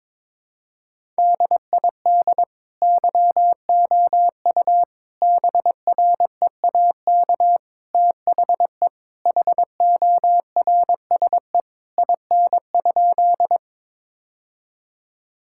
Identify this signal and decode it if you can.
Morse code, heard as a keyed tone, about 22 wpm, 700 Hz. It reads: DID YOU BREAK THE HORSE IN?
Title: Sentences from Top 1000 Words - Review Part 6 - 22wpm